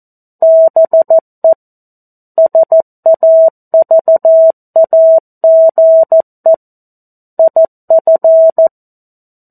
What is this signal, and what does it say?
Morse code, heard as a keyed tone, about 14 wpm, 650 Hz. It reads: BE SAVAGE IF